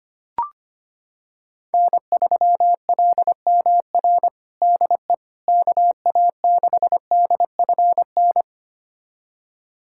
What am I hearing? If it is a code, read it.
Morse code, heard as a keyed tone, about 25 wpm, 700 Hz. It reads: N3LMR DE KA6DFN
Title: USA Callsign DE Callsign 25wpm